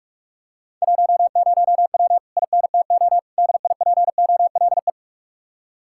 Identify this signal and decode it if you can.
Morse code, heard as a keyed tone, about 47 wpm, 700 Hz. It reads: 10W INTO DIPOLE